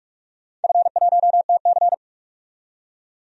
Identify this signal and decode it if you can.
Morse code, heard as a keyed tone, about 42 wpm, 700 Hz. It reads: F1TC
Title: DXCC-100 Call Signs - 4 Characters - 45wpm